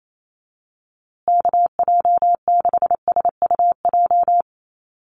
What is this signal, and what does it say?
Morse code, heard as a keyed tone, about 28 wpm, 700 Hz. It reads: KJ6SUJ
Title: USA Call Signs Repeated 2x 28wpm